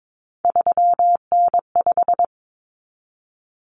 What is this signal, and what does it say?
Morse code, heard as a keyed tone, about 22 wpm, 700 Hz. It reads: 3N5
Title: Common Cut Numbers - 3-Digit 22wpm